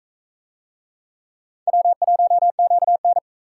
Morse code, heard as a keyed tone, about 44 wpm, 700 Hz. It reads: W1QN